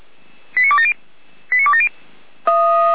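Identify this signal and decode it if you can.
Five-tone selective call (ZVEI), heard as quick runs of five tones, tones of about 70 ms. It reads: 89270 89270
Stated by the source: ZVEI five-tone call received by SDR.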